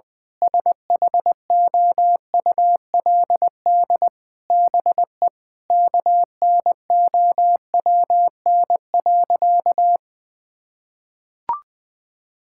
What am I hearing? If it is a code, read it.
Morse code, heard as a keyed tone, about 20 wpm, 700 Hz. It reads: SHOULD BE KNOWN.
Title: Sentences from Top 1000 Words - Review Part 6 - 20wpm